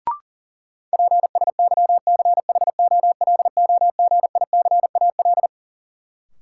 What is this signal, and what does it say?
Morse code, heard as a keyed tone, about 40 wpm, 700 Hz. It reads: PSYCHOLOGICAL
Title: Sentences from Top 1400 Words - New Word Drill - 40wpm